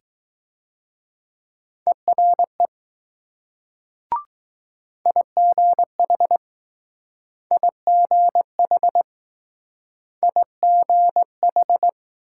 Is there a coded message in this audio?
Yes: Morse code from 1.9 to 11.9 s, heard as a keyed tone, about 19 wpm, 700 Hz. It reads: ERE IGH IGH IGH